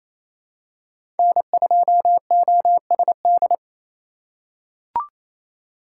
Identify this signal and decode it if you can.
Morse code, heard as a keyed tone, about 28 wpm, 700 Hz. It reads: N2OSD
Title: USA Call Signs - 5 Characters - 2x Character Spacing - 28wpm